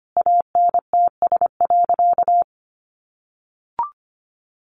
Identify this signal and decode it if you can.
Morse code, heard as a keyed tone, about 25 wpm, 700 Hz. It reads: ANTS.
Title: Easy Phrases 25wpm